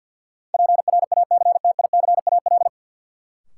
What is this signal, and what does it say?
Morse code, heard as a keyed tone, about 53 wpm, 700 Hz. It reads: PRACTICAL
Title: Sentences from Top 1400 Words - New Word Drill - 50wpm